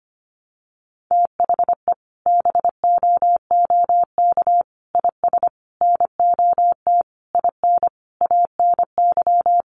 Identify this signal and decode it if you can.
Morse code, heard as a keyed tone, about 25 wpm, 700 Hz. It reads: THE BOOK IS NOT IN ANY
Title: Sentences from Top 1300 Words 25wpm